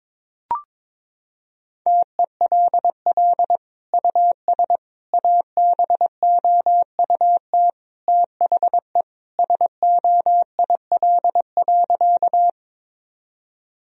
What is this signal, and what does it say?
Morse code, heard as a keyed tone, about 22 wpm, 700 Hz. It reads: TELL US ABOUT THE SOIL.
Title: Sentences from Top 1000 Words - Review Part 4 - 22wpm